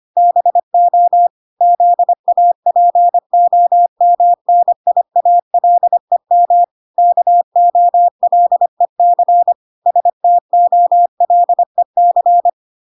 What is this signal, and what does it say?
Morse code, heard as a keyed tone, about 25 wpm, 700 Hz. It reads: BO ZAPOMNIALEM KOLEC STOLEC